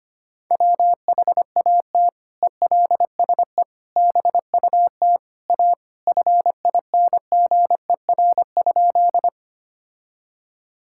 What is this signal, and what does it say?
Morse code, heard as a keyed tone, about 25 wpm, 700 Hz. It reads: WHAT ELSE BUT A FINGER?
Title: Sentences from Top 1600 Words 25wpm